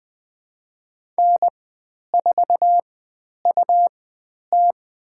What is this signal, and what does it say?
Morse code, heard as a keyed tone, about 20 wpm, spaced as for 10 wpm, 700 Hz. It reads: N4UT
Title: 109 Lesson 18g - Letter m - Call Signs 20wpm